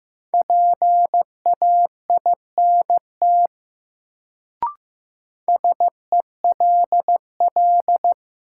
Morse code, heard as a keyed tone, about 15 wpm, 700 Hz. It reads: PAINT SELL
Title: Sentences from Top 1000 Words - Review Part 6 - New Word Drill - 15wpm